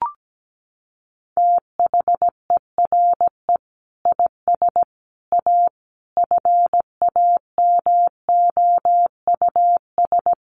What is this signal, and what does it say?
Morse code, heard as a keyed tone, about 17 wpm, 700 Hz. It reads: THERE IS A FAMOUS
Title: Sentences from Top 1500 Words 17wpm